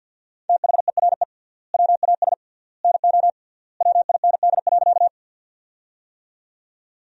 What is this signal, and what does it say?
Morse code, heard as a keyed tone, about 50 wpm, 700 Hz. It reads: THERE WAS NO WIND.
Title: 3x Word-Spacing - Sentences from Top 400 Words - 50wpm